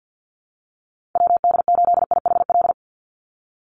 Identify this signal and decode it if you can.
Morse code, heard as a keyed tone, about 50 wpm, 700 Hz. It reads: WD8IHL
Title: USA Call Signs 50wpm